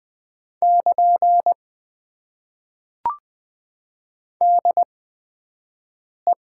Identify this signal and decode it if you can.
Morse code, heard as a keyed tone, about 20 wpm, spaced as for 6 wpm, 700 Hz. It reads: <KN> DE
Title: Top 90 QSO Elements - 8x Character Spacing - 20wpm